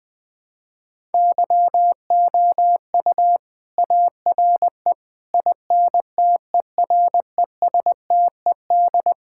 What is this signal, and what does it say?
Morse code, heard as a keyed tone, about 20 wpm, 700 Hz. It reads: YOU ARE INTERESTED
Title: Sentences from Top 1000 Words - Review Part 6 - 20wpm